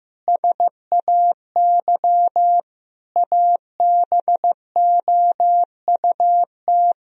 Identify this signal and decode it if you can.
Morse code, heard as a keyed tone, about 16 wpm, 700 Hz. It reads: SAY ABOUT
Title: Sentences from Top 1000 Words - Review Part 3 - 15wpm